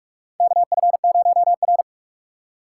Morse code, heard as a keyed tone, about 46 wpm, 700 Hz. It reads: KF0R
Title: USA Call Signs - 4 Characters - 6x Character Spacing - 45wpm